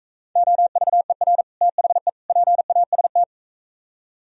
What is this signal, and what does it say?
Morse code, heard as a keyed tone, about 45 wpm, 700 Hz. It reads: OVER THE PAST